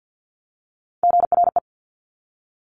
Morse code, heard as a keyed tone, about 50 wpm, 700 Hz. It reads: GRE
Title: Top Three Letter Combinations in Words 50wpm